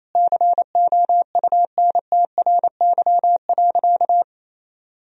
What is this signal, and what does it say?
Morse code, heard as a keyed tone, about 28 wpm, 700 Hz. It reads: COUNTRY.
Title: Sentences from Top 1000 Words - Review Part 3 - 28wpm